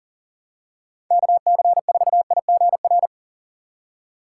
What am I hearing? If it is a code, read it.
Morse code, heard as a keyed tone, about 40 wpm, 700 Hz. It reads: KC4IGR
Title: USA Call Signs - 6 Characters - 3x Character Spacing - 40wpm